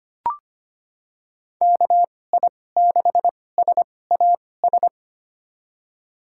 Morse code, heard as a keyed tone, about 25 wpm, spaced as for 18 wpm, 700 Hz. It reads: KI6SAS
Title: USA Call Signs - 2x Character Spacing - 25wpm